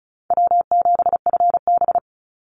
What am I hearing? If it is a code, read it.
Morse code, heard as a keyed tone, about 34 wpm, 700 Hz. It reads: W7FB
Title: Easy Call Signs Repeated 2x - 35wpm